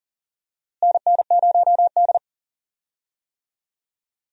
Morse code, heard as a keyed tone, about 40 wpm, 700 Hz. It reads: NN0D
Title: USA Call Signs - 4 Characters - Repeated 2x - 40wpm